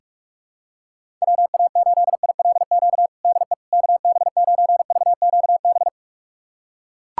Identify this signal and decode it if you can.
Morse code, heard as a keyed tone, about 41 wpm, 700 Hz. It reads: WA8ILQ DE KB9VQB
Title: USA Callsign DE Callsign 45wpm